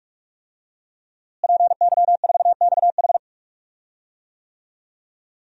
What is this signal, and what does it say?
Morse code, heard as a keyed tone, about 44 wpm, 700 Hz. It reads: PY4XH
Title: DXCC-100 Call Signs - 5 Characters - Repeated 3x - 45wpm